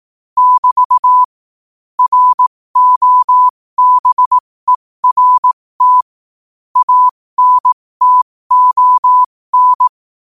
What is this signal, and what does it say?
Morse code, heard as a keyed tone, about 18 wpm, spaced as for 15 wpm, 1000 Hz. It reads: <BT> ROBERT ANTON